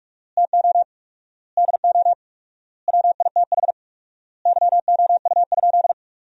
Morse code, heard as a keyed tone, about 46 wpm, 700 Hz. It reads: TO DO WITH YOU?
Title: 4x Word-Spacing - Sentences from Top 100 Words - 45wpm